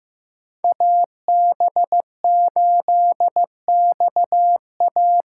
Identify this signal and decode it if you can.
Morse code, heard as a keyed tone, about 15 wpm, 700 Hz. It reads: AB8XA
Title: USA Call Signs - 6x Character Spacing - 15wpm